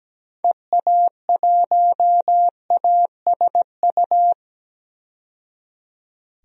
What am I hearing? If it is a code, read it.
Morse code, heard as a keyed tone, about 17 wpm, 700 Hz. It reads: EA1ASU
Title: DXCC-100 Call Signs - 6 Characters - 17wpm